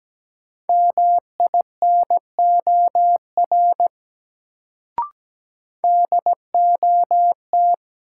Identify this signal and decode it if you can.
Morse code, heard as a keyed tone, about 17 wpm, 700 Hz. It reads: MINOR DOT
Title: Sentences from Top 1500 Words - New Word Drill - 17wpm